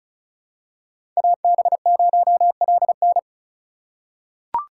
Morse code, heard as a keyed tone, about 36 wpm, 700 Hz. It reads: AB0LN
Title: USA Call Signs - 5 Characters - 3x Character Spacing - 35wpm